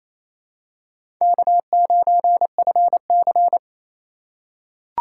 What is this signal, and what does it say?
Morse code, heard as a keyed tone, about 28 wpm, 700 Hz. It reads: K9FC
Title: USA Call Signs - 2x Character Spacing - 28wpm